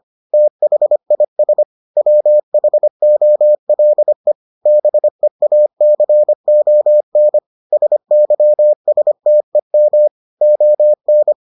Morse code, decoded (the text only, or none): THIS WHOLE BEACON SYSTEM ON